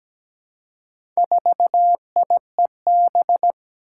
Morse code, heard as a keyed tone, about 17 wpm, 700 Hz. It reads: 4IEB
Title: Four Challenging Letter-Number Combinations 17wpm